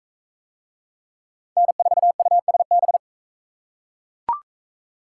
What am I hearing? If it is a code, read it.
Morse code, heard as a keyed tone, about 39 wpm, 700 Hz. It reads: N4USB